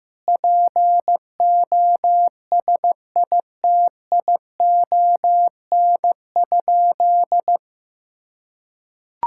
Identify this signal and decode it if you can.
Morse code, heard as a keyed tone, about 15 wpm, 700 Hz. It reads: POSITION?
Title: Sentences from Top 1000 Words - Review Part 5 - 15wpm